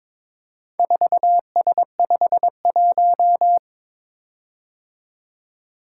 Morse code, heard as a keyed tone, about 22 wpm, 700 Hz. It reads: 4S51
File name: practice / 22wpm / Four Letter-Number Combinations - 22wpm